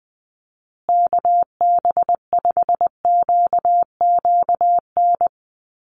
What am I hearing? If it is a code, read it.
Morse code, heard as a keyed tone, about 20 wpm, 700 Hz. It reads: KB5QQN
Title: USA Call Signs Repeated 3x 20wpm